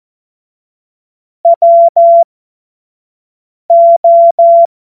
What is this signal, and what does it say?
Morse code, heard as a keyed tone, about 14 wpm, spaced as for 5 wpm, 680 Hz. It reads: WO
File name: Witz-09-September-2025-25-F70